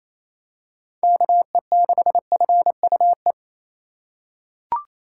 Morse code, heard as a keyed tone, about 28 wpm, 700 Hz. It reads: KE6FUE
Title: USA Call Signs - 6 Characters - 4x Character Spacing - 28wpm